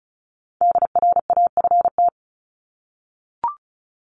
Morse code, heard as a keyed tone, about 34 wpm, 700 Hz. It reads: DRAFT
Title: Top 1751-2000 Words 35wpm